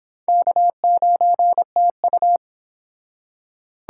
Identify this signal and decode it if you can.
Morse code, heard as a keyed tone, about 29 wpm, 700 Hz. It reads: K9TU